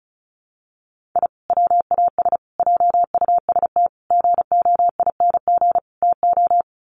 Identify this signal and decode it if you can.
Morse code, heard as a keyed tone, about 34 wpm, 700 Hz. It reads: I WAS JUST GOING TO
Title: Sets of 5 Words Encore 35wpm